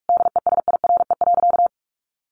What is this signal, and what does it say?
Morse code, heard as a keyed tone, about 44 wpm, 700 Hz. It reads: DESIRE.
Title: Sentences from Top 1100 Words 45wpm